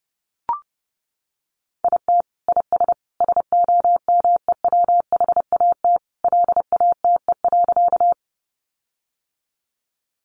Morse code, heard as a keyed tone, about 30 wpm, 700 Hz. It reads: IT IS SOMEWHAT LATE.
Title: Sentences from Top 1300 Words 30wpm